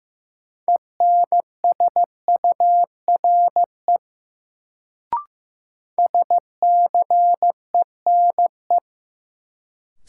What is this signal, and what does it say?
Morse code, heard as a keyed tone, about 15 wpm, 700 Hz. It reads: ENSURE SCENE
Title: Sentences from Top 1100 Words - New Word Drill Alt - 15wpm